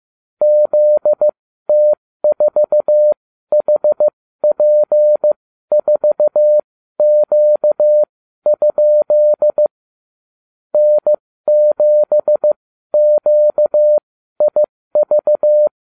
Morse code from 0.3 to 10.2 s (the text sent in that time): ZT4HP4Q?